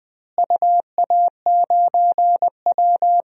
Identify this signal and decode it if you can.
Morse code, heard as a keyed tone, about 20 wpm, 700 Hz. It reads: UA9W
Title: DXCC-100 Call Signs - 4 Characters - Repeated 2x - 20wpm